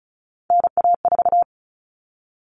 Morse code, heard as a keyed tone, about 34 wpm, 700 Hz. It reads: NA4
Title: All Cut Numbers - 3-Digit 35wpm